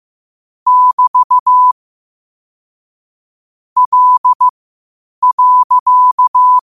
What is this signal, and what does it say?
Morse code, heard as a keyed tone, about 15 wpm, spaced as for 8 wpm, 1000 Hz. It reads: <BT> L.
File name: qotdfiles / 8wpm